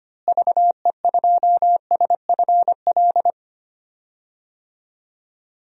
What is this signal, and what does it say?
Morse code, heard as a keyed tone, about 25 wpm, 700 Hz. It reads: VE2SFL
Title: DXCC-100 Call Signs - 6 Characters - Repeated 3x - 25wpm